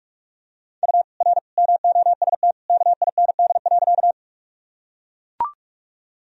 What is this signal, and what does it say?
Morse code, heard as a keyed tone, about 42 wpm, 700 Hz. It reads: U R MOST KIND.
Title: Sentences from Top 200 Words and Abbreviations 45wpm